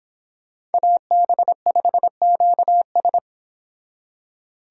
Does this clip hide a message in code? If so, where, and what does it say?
Yes: Morse code from 0.7 to 3.2 s, heard as a keyed tone, about 26 wpm, 700 Hz. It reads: AB5QS